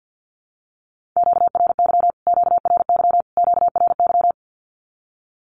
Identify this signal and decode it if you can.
Morse code, heard as a keyed tone, about 50 wpm, 700 Hz. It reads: QRY QRY QRY
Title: Q-Codes Repeated 3x 50wpm